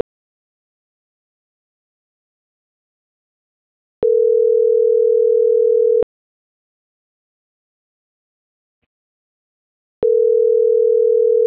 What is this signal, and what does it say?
Ringback tone (North America), 440 + 480 Hz.